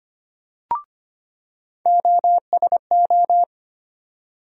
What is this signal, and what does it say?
Morse code, heard as a keyed tone, about 25 wpm, 700 Hz. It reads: OSO
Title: USA Call Sign Postfixes - 3 Letters 25wpm